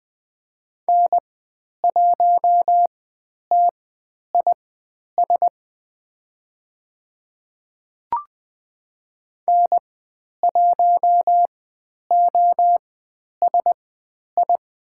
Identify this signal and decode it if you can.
Morse code, heard as a keyed tone, about 20 wpm, spaced as for 10 wpm, 700 Hz. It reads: N1TIS N1OSI
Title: 102 Lesson 17g - Letter u - Call Signs 20wpm